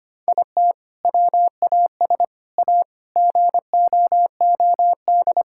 Morse code, heard as a keyed tone, about 25 wpm, 700 Hz. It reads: IT WAS A GOOD
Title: Sentences from Top 1000 Words - Review Part 5 - 25wpm